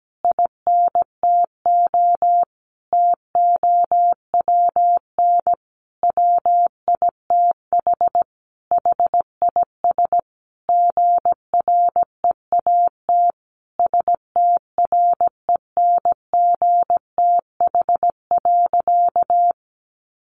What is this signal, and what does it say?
Morse code, heard as a keyed tone, about 17 wpm, 700 Hz. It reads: INTO TOWN WITH HIS GREAT STRENGTH.
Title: Sentences from Top 1300 Words 17wpm